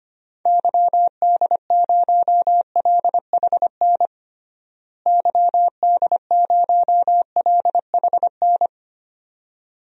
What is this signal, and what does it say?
Morse code, heard as a keyed tone, about 25 wpm, 700 Hz. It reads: YD0LHN YD0LHN
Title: DXCC-100 Call Signs - 6 Characters - Repeated 2x - 25wpm